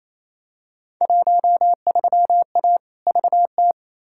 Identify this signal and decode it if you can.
Morse code, heard as a keyed tone, about 28 wpm, 700 Hz. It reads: 13A VT
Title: Field Day Exchanges Encore - 28wpm